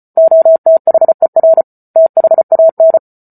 Morse code, heard as a keyed tone, about 34 wpm, 650 Hz. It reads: OTHER THAN